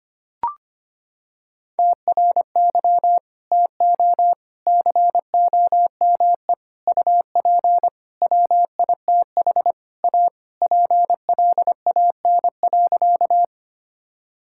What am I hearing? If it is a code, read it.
Morse code, heard as a keyed tone, about 25 wpm, 700 Hz. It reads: TRY TO COME UP WITH A PLAN.
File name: practice / 25wpm / Sentences with 4-Word Phrases - 25wpm